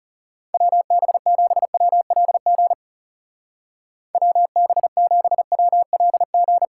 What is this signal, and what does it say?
Morse code, heard as a keyed tone, about 38 wpm, 700 Hz. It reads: WB7WLG WB7WLG